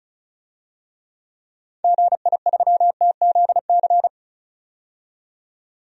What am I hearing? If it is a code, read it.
Morse code, heard as a keyed tone, about 34 wpm, 700 Hz. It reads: GI3TZC
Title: DXCC-100 Call Signs - 6 Characters - 35wpm